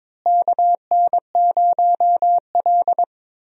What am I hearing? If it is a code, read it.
Morse code, heard as a keyed tone, about 21 wpm, 700 Hz. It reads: KN0L
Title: USA Call Signs - 4 Characters - Repeated 3x - 22wpm